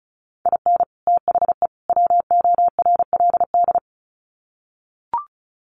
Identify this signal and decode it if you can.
Morse code, heard as a keyed tone, about 34 wpm, 700 Hz. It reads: IN THE WORLD
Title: Sets of 3 Words 35wpm